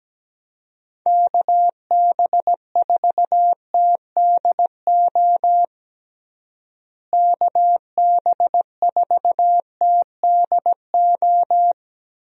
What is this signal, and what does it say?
Morse code, heard as a keyed tone, about 17 wpm, 700 Hz. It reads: KB4TDO KB4TDO
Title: USA Call Signs - 6 Characters - Repeated 3x - 17wpm